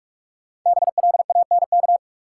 Morse code, heard as a keyed tone, about 46 wpm, 700 Hz. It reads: BLANK
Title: Sentences from Top 1500 Words - New Word Drill - 45wpm